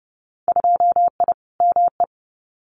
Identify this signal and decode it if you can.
Morse code, heard as a keyed tone, about 30 wpm, 700 Hz. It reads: 2I ME
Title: Winter Field Day Exchanges Encore 30wpm